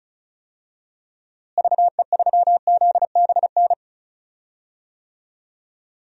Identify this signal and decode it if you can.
Morse code, heard as a keyed tone, about 36 wpm, 700 Hz. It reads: VE3ZBN
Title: DXCC-100 Call Signs - 6 Characters - Repeated 2x - 35wpm